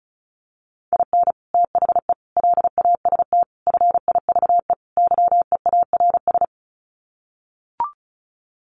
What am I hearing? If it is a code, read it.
Morse code, heard as a keyed tone, about 34 wpm, 700 Hz. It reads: IN THE LAST FIVE YEARS
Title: Sets of 5 Words Encore 35wpm